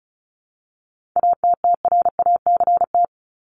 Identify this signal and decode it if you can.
Morse code, heard as a keyed tone, about 34 wpm, 700 Hz. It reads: ATTRACT
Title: Top 1501-1750 Words 35wpm